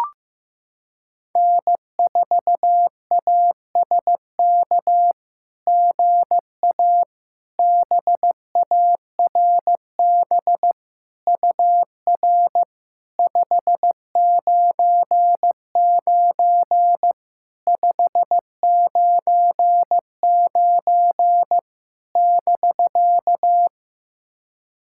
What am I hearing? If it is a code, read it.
Morse code, heard as a keyed tone, about 15 wpm, 700 Hz. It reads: N4ASK GA BARB UR 599 599 <BK>
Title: SOTA QSO Practice 15wpm